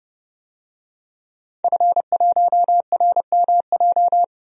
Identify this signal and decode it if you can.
Morse code, heard as a keyed tone, about 30 wpm, 700 Hz. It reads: F1RMJ
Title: DXCC-100 Call Signs - 5 Characters - 30wpm